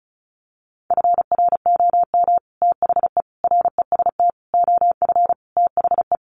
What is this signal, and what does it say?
Morse code, heard as a keyed tone, about 34 wpm, 700 Hz. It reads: FROM THE REST OF THE
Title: Sets of 5 Words 35wpm